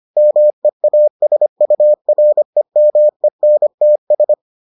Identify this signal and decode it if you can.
Morse code, heard as a keyed tone, about 25 wpm, 600 Hz. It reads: MEASUREMENTS